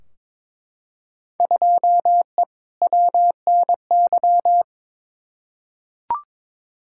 Morse code, heard as a keyed tone, about 22 wpm, 700 Hz. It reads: 2E WNY
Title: Field Day Exchanges Encore - 22wpm